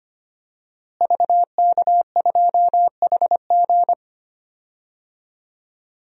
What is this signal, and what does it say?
Morse code, heard as a keyed tone, about 25 wpm, 700 Hz. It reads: VK2HG
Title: DXCC-100 Call Signs - 5 Characters - Repeated 3x - 25wpm